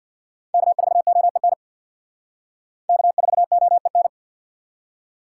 Morse code, heard as a keyed tone, about 64 wpm, 700 Hz. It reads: K4OEN K4OEN